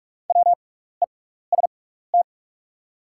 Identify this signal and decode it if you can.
Morse code, heard as a keyed tone, about 44 wpm, spaced as for 15 wpm, 690 Hz. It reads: WEST
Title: 6x Character Spacing - Top 500 Words - 45wpm